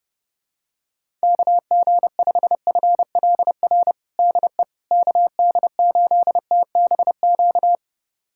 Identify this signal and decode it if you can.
Morse code, heard as a keyed tone, about 32 wpm, 700 Hz. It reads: KG5FLR DE KD8TBQ
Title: USA Callsign DE Callsign 30wpm